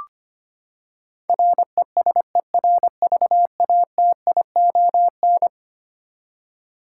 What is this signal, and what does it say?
Morse code, heard as a keyed tone, about 25 wpm, 700 Hz. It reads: RESERVATION
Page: morse-code-ninja-practice-25wpm